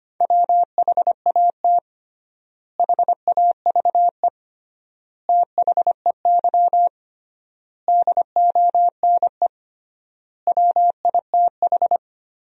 Morse code, heard as a keyed tone, about 25 wpm, 700 Hz. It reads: WHAT HAVE THEY DONE WITH
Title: 3x Word-Spacing - Sentences from Top 500 Words - 25wpm